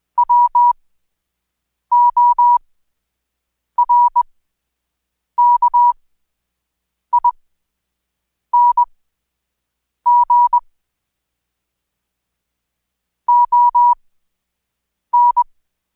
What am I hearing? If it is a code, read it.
Morse code, heard as a keyed tone, about 21 wpm, spaced as for 6 wpm, 970 Hz. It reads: WORKING ON
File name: Code Quick/AudioDownld2/Seg109-Day14-DayDrill 14